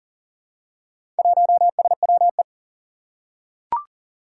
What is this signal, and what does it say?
Morse code, heard as a keyed tone, about 40 wpm, 700 Hz. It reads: 1SWE
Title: Four Challenging Letter-Number Combinations 40wpm